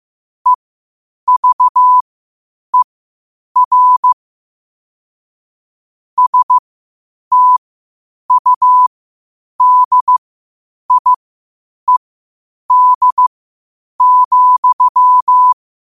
Morse code, heard as a keyed tone, about 15 wpm, spaced as for 8 wpm, 1000 Hz. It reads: EVER STUDIED,
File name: qotdfiles / 8wpm